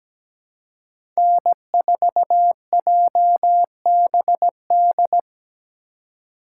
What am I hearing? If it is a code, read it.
Morse code, heard as a keyed tone, about 17 wpm, 700 Hz. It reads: N4JBD
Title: USA Call Signs - 5 Characters - 4x Character Spacing - 17wpm